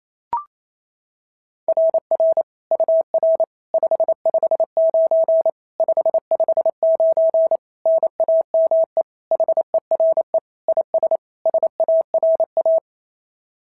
Morse code, heard as a keyed tone, about 27 wpm, 650 Hz. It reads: RR UR 559 559 NAME HERE IS SARA
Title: SOTA QSO Practice 28wpm